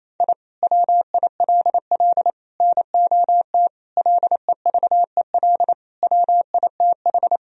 Morse code, heard as a keyed tone, about 28 wpm, 700 Hz. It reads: I WILL NOT LEVEL WITH
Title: Sentences from Top 1000 Words - Review Part 4 - 28wpm